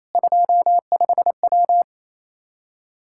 Morse code, heard as a keyed tone, about 28 wpm, 700 Hz. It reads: 25W
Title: Practice QSO - Exchange 2 - Info - 28wpm